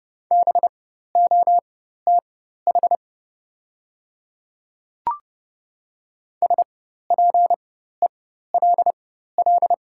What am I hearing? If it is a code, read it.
Morse code, heard as a keyed tone, about 30 wpm, spaced as for 14 wpm, 700 Hz. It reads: BOTH SPELL
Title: 4x Character Spacing - Top 300 Words - 30wpm